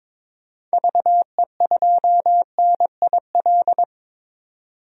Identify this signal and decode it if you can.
Morse code, heard as a keyed tone, about 21 wpm, 700 Hz. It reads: VE2NIL
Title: DXCC-100 Call Signs - 6 Characters - 22wpm